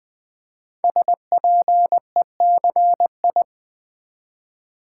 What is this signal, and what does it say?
Morse code, heard as a keyed tone, about 20 wpm, 700 Hz. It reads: SPECI
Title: Build A Word - 5 Letters At A Time - 12-Letter Words - 20wpm